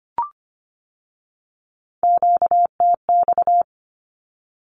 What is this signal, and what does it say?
Morse code, heard as a keyed tone, about 25 wpm, 700 Hz. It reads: QTX
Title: Q-Codes with Spoken Meaning - 25wpm